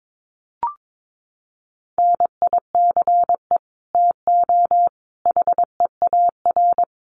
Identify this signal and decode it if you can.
Morse code, heard as a keyed tone, about 22 wpm, 700 Hz. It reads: NICE TO HEAR
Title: Common QSO Phrases 22wpm